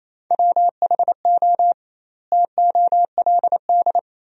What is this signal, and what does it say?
Morse code, heard as a keyed tone, about 28 wpm, 700 Hz. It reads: WHO TOLD
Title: 2x Word-Spacing - Sentences from Top 400 Words - 28wpm